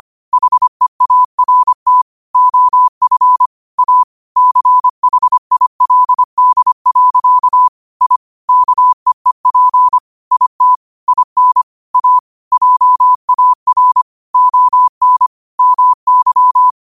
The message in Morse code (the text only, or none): HEART OF A CHILD. I KEEP IT IN A JAR ON MY